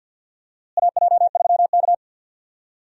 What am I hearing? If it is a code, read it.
Morse code, heard as a keyed tone, about 52 wpm, 700 Hz. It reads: AJ3K
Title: USA Call Signs - 4 Characters - Repeated 3x - 50wpm